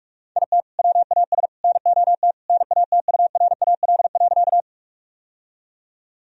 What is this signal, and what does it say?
Morse code, heard as a keyed tone, about 45 wpm, 700 Hz. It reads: IT WAS NOT NATURAL.
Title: Sentences from Top 1000 Words - Review Part 4 - 45wpm